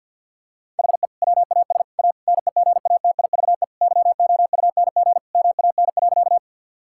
Morse code, heard as a keyed tone, about 47 wpm, 700 Hz. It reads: HE WAS A NEGATIVE YOUNG MAN.
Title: Sentences from Top 1000 Words - Review Part 6 - 50wpm